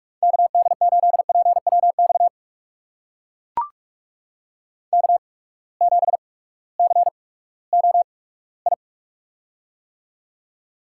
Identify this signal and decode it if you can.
Morse code, heard as a keyed tone, about 44 wpm, spaced as for 20 wpm, 700 Hz. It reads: KD8PWX K7COI
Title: USA Call Signs - 8x Character Spacing - 45wpm